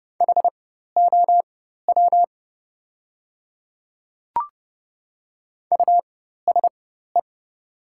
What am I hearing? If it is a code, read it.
Morse code, heard as a keyed tone, about 30 wpm, spaced as for 14 wpm, 700 Hz. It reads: HOW USE
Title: 4x Character Spacing - Top 100 Words - 30wpm